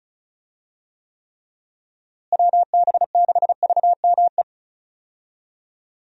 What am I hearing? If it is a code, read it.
Morse code, heard as a keyed tone, about 36 wpm, 700 Hz. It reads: WB6VME
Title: USA Call Signs - 6 Characters - Repeated 2x - 35wpm